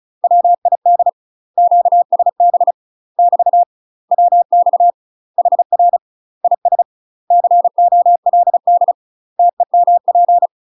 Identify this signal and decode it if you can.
Morse code, heard as a keyed tone, about 33 wpm, 700 Hz. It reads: WID QSB <BT> WX HR IS COLD TEMP
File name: UK MASTER QSOs TRAINING / 35WPM / UK MASTER QSOs 0026